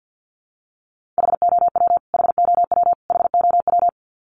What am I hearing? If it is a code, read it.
Morse code, heard as a keyed tone, about 50 wpm, 700 Hz. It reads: HOW HOW HOW
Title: Top 200 Words Repeated 50wpm